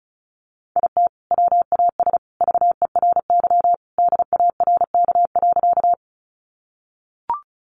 Morse code, heard as a keyed tone, about 34 wpm, 700 Hz. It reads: IT WAS VERY DARK.
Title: Sentences from Top 500 Words 35wpm